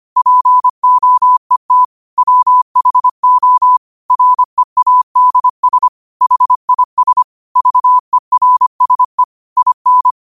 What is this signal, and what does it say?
Morse code, heard as a keyed tone, about 25 wpm, 1000 Hz. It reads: POET WHO READS HIS VERSE IN
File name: qotdfiles / 25wpm